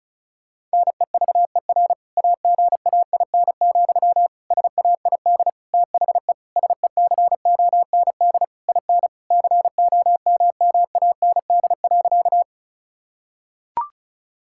Morse code, heard as a keyed tone, about 36 wpm, 700 Hz. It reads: NEVER AGAIN, SAID THE SECOND IN COMMAND.
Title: Sentences from Top 1600 Words 35wpm